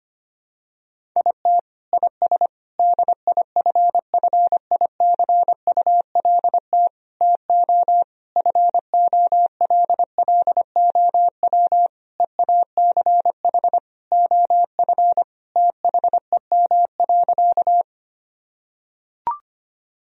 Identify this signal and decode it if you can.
Morse code, heard as a keyed tone, about 25 wpm, 700 Hz. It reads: IT IS DIFFICULT TO FOLLOW EACH OF THEM.
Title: Sentences from Top 1000 Words - Review Part 1 - 25wpm